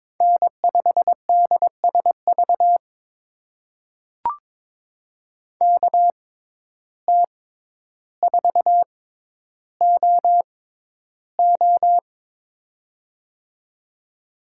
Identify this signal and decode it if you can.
Morse code, heard as a keyed tone, about 22 wpm, spaced as for 11 wpm, 700 Hz. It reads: N5DSV KT4OO
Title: USA Call Signs - 5 Characters - 6x Character Spacing - 22wpm